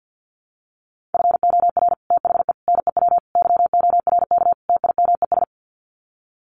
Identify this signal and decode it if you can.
Morse code, heard as a keyed tone, about 50 wpm, 700 Hz. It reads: FOR THE NEW YORK TIMES
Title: Sets of 5 Words 50wpm